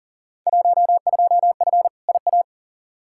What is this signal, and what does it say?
Morse code, heard as a keyed tone, about 40 wpm, 700 Hz. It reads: 12F IA
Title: Field Day Exchanges Encore - 40wpm